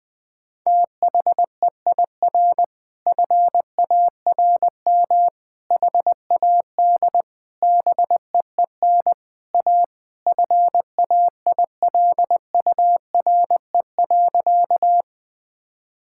Morse code, heard as a keyed tone, about 19 wpm, 700 Hz. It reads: THEIR FARM HAD BEEN A FAILURE.
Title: Sentences from Top 1000 Words - Review Part 6 - 20wpm